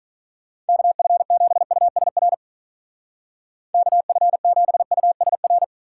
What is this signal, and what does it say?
Morse code, heard as a keyed tone, about 42 wpm, 700 Hz. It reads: KF7USR KF7USR
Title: Speed Racing - USA 2x3 Call Signs 35wpm